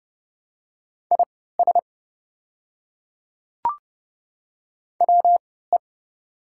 Morse code, heard as a keyed tone, about 32 wpm, spaced as for 17 wpm, 700 Hz. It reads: IS WE
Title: 3x Character Spacing - Top 100 Words - 30wpm